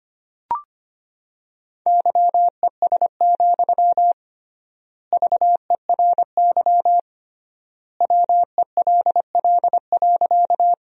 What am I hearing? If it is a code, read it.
Morse code, heard as a keyed tone, about 25 wpm, 700 Hz. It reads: YES, VERY WELL.
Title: 3x Word-Spacing - Sentences from Top 500 Words - 25wpm